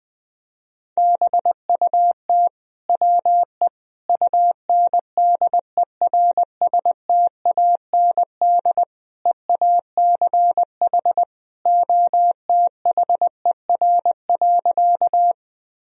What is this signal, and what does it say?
Morse code, heard as a keyed tone, about 20 wpm, 700 Hz. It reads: BUT WE UNDERSTAND EACH OTHER.
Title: Sentences from Top 1000 Words - Review Part 1 - 20wpm